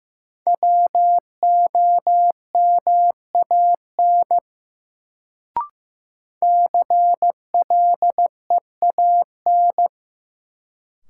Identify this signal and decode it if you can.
Morse code, heard as a keyed tone, about 15 wpm, 700 Hz. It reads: WOMAN CLEAN
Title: Sentences from Top 1100 Words - New Word Drill Alt - 15wpm